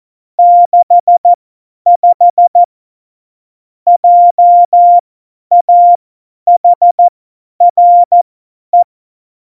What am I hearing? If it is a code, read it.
Morse code, heard as a keyed tone, about 14 wpm, spaced as for 10 wpm, 700 Hz. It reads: 65 JAHRE
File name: Morsekurs / Witze / Witz 20. Oktober 2020, 50 F70